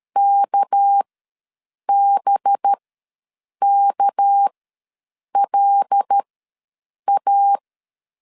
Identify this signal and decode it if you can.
Morse code, heard as a keyed tone, about 13 wpm, spaced as for 7 wpm, 800 Hz. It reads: KBKLA
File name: LEZIONI / 6 PKX0 / 6 PK ws 12wpm 3116